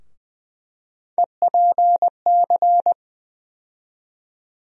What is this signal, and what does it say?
Morse code, heard as a keyed tone, about 20 wpm, 700 Hz. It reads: EPC
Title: USA Call Sign Postfixes - 3 Letters 20wpm